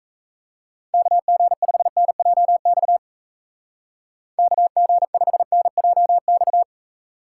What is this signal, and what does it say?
Morse code, heard as a keyed tone, about 36 wpm, 700 Hz. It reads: KG5NJX KG5NJX